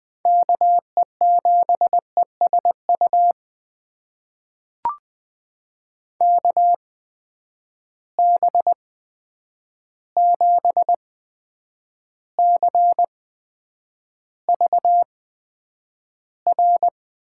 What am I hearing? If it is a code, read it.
Morse code, heard as a keyed tone, about 20 wpm, spaced as for 9 wpm, 700 Hz. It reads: KE7ESU KB7CVR
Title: USA Call Signs - 6 Characters - 8x Character Spacing - 20wpm